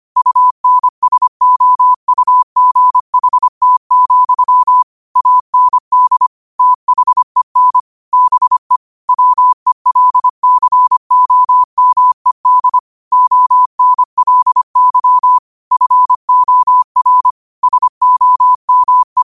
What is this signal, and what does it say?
Morse code, heard as a keyed tone, about 25 wpm, 1000 Hz. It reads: UNSOUGHT, AND THEN BE WELCOMED ONLY FOR SOME
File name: qotdfiles / 25wpm